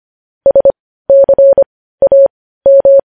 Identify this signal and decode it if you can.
Morse code, heard as a keyed tone, about 25 wpm, spaced as for 15 wpm, 550 Hz. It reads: SCAM
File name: podcast / News headlines-1x-25x15